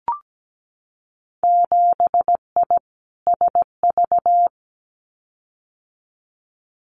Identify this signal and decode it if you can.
Morse code, heard as a keyed tone, about 17 wpm, 700 Hz. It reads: 7I SV
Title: Winter Field Day Exchanges Encore 17wpm